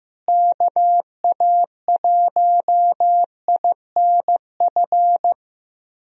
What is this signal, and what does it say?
Morse code, heard as a keyed tone, about 15 wpm, 700 Hz. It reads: KA1INF
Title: USA Call Signs - 6 Characters - Repeated 3x - 15wpm